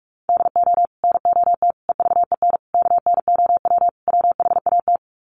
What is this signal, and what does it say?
Morse code, heard as a keyed tone, about 44 wpm, 700 Hz. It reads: DO NOT EVEN KNOW WHAT
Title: Sets of 5 Words Encore 45wpm